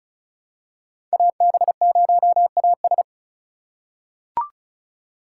Morse code, heard as a keyed tone, about 34 wpm, 700 Hz. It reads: AB0AS
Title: USA Call Signs - 3x Character Spacing - 35wpm